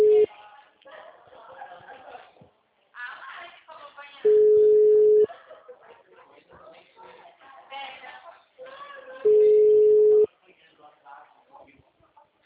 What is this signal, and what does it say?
Ringback tone (Europe, 425 Hz), with speech.
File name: out- -2011- - - .61356